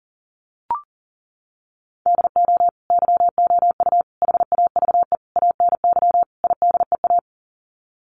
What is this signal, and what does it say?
Morse code, heard as a keyed tone, about 40 wpm, 700 Hz. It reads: DO YOU HAVE ANY IDEA
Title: Sets of 5 Words Encore 40wpm